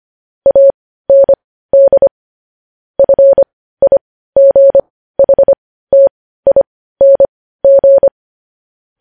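Morse code, heard as a keyed tone, about 25 wpm, spaced as for 15 wpm, 550 Hz. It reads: AND FIGHTING
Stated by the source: Morse news headlines practice at 15.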